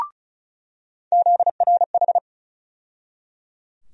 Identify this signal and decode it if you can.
Morse code, heard as a keyed tone, about 34 wpm, 700 Hz. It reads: ZRH
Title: USA Call Sign Postfixes - 3 Letters 35wpm